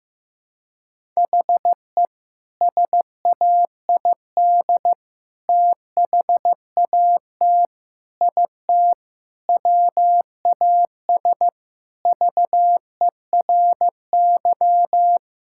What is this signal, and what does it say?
Morse code, heard as a keyed tone, about 15 wpm, 700 Hz. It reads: HE SAID THAT IT WAS VERY
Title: Sentences from Top 1000 Words - Review Part 6 - 15wpm